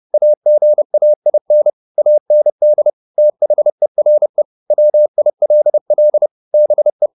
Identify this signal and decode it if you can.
Morse code, heard as a keyed tone, about 30 wpm, 600 Hz. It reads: AGAIN AND THERE WILL BE